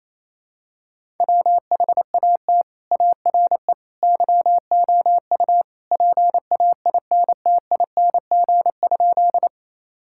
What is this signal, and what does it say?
Morse code, heard as a keyed tone, about 28 wpm, 700 Hz. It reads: WHAT ARE YOU PAINTING?
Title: Sentences from Top 1000 Words - Review Part 6 - 28wpm